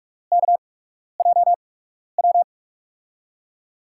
Morse code, heard as a keyed tone, about 44 wpm, spaced as for 12 wpm, 700 Hz. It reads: KJW
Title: USA Call Signs - 5 Characters - 8x Character Spacing - 45wpm